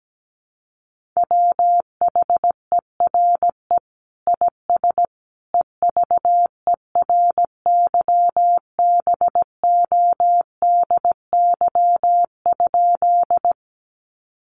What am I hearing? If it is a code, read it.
Morse code, heard as a keyed tone, about 17 wpm, 700 Hz. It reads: WHERE IS EVERYBODY?
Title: Sentences from Top 1400 Words 17wpm